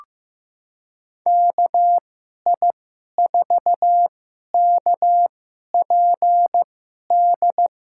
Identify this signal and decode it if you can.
Morse code, heard as a keyed tone, about 15 wpm, spaced as for 11 wpm, 700 Hz. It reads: KI4KPD
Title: USA Call Signs - 6 Characters - 2x Character Spacing - 15wpm